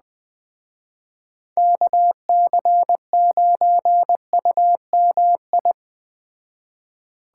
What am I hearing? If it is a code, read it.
Morse code, heard as a keyed tone, about 20 wpm, 700 Hz. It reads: KC9UMI
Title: Speed Racing - USA 2x3 Call Signs 17wpm